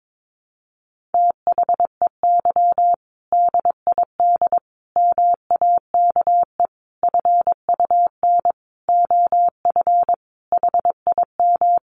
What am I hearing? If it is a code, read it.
Morse code, heard as a keyed tone, about 22 wpm, 700 Hz. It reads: THEY DID MAKE FUN OF HIM
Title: Sentences from Top 1300 Words 22wpm